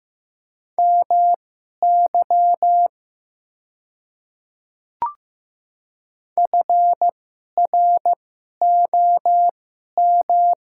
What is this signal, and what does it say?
Morse code, heard as a keyed tone, about 15 wpm, spaced as for 11 wpm, 700 Hz. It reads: MY FROM
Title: 2x Character Spacing - Top 100 Words - 15wpm